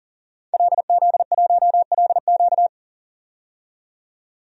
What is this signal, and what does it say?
Morse code, heard as a keyed tone, about 40 wpm, 700 Hz. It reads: LZ1LQ
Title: DXCC-100 Call Signs - 5 Characters - 40wpm